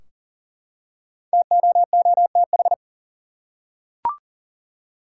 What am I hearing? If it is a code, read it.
Morse code, heard as a keyed tone, about 40 wpm, 700 Hz. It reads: TOOTH
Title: Five Letter Words - 40wpm